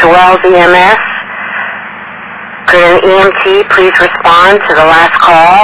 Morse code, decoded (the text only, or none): none